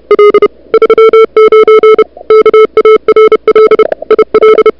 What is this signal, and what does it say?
Morse code, heard as a keyed tone, about 33 wpm, 440 Hz, with noise. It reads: L 39 KARL IL
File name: CWSprint / February 2007 CW Sprint / 01-2007 CW Sprint Hour1